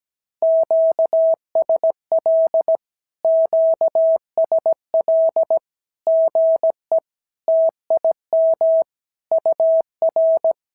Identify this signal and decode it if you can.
Morse code, heard as a keyed tone, about 17 wpm, 650 Hz. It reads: QSL QSL GE TIM UR
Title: SOTA QSO Practice 17wpm